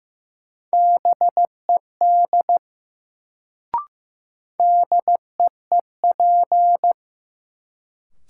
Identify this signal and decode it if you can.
Morse code, heard as a keyed tone, about 15 wpm, 700 Hz. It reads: BED DEEP
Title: Sentences from Top 500 Words - New Word Drill - 15wpm